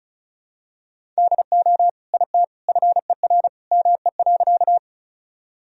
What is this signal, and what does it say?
Morse code, heard as a keyed tone, about 36 wpm, 700 Hz. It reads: DO IT FER ME.
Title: Sentences from Top 200 Words and Abbreviations 35wpm